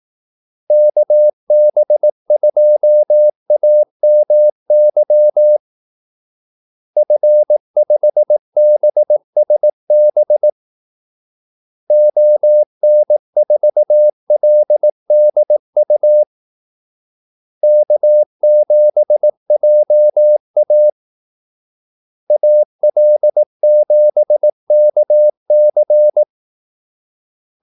Morse code, heard as a keyed tone, about 18 wpm, 600 Hz. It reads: KB2AMY F5BSB ON4LDU K7JA AL7KC
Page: Kaldesignaler 18 wpm